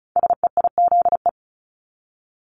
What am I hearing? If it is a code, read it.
Morse code, heard as a keyed tone, about 35 wpm, 700 Hz. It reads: SEIZE
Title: Multi-letter Phonemes Repeated 3x 35wpm